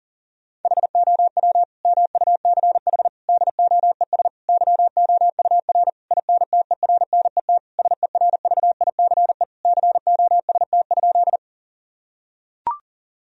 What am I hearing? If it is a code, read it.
Morse code, heard as a keyed tone, about 40 wpm, 700 Hz. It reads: HOW MUCH DOES YOUR INTERNET SERVICE COST?
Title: Sentences from Top 1000 Words - Review Part 4 - 40wpm